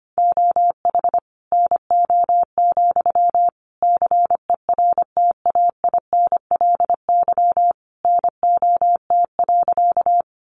Morse code, heard as a keyed tone, about 25 wpm, 700 Hz. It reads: OH NO, CERTAINLY NOT.
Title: Sentences from Top 900 Words 25wpm